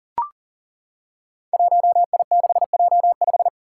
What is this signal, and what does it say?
Morse code, heard as a keyed tone, about 37 wpm, 690 Hz. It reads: 1I6J5
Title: Five Challenging Letter-Number Combinations 40wpm